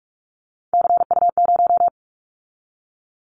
Morse code, heard as a keyed tone, about 44 wpm, 700 Hz. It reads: CU0
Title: Callsign Prefixes for Top 100 DXCC - 45wpm